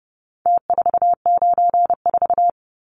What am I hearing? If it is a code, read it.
Morse code, heard as a keyed tone, about 30 wpm, 700 Hz. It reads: T494
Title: Numbers - Contest Style - 4-Digit 30wpm